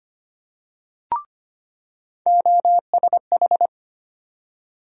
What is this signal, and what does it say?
Morse code, heard as a keyed tone, about 25 wpm, 700 Hz. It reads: OSH
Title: USA Call Sign Postfixes - 3 Letters 25wpm